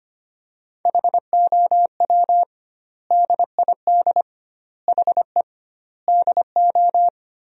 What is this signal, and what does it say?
Morse code, heard as a keyed tone, about 25 wpm, 700 Hz. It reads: HOW DID HE DO
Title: 2x Word-Spacing - Sentences from Top 100 Words - 25wpm